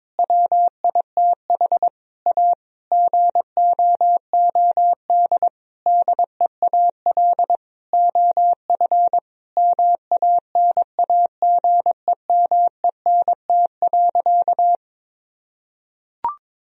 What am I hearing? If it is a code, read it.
Morse code, heard as a keyed tone, about 21 wpm, 700 Hz. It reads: WITH A GOOD DEAL OF MANAGEMENT.
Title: Sentences from Top 1000 Words - Review Part 4 - 22wpm